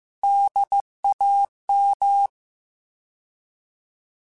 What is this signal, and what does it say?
Morse code, heard as a keyed tone, about 15 wpm, 800 Hz. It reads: DAM
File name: Alphabetical / Alphabetical mp3s / Ds @13